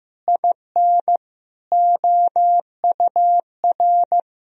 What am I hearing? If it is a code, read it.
Morse code, heard as a keyed tone, about 15 wpm, 700 Hz. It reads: IN OUR
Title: Sentences from Top 1000 Words - Review Part 4 - 15wpm